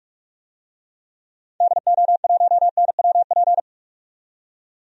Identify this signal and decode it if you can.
Morse code, heard as a keyed tone, about 46 wpm, 700 Hz. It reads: DO1NWP
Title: DXCC-100 Call Signs - 6 Characters - 45wpm